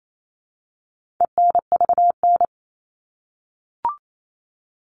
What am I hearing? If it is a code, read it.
Morse code, heard as a keyed tone, about 28 wpm, 700 Hz. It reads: ENVN
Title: All Cut Numbers - 4-Digit 28wpm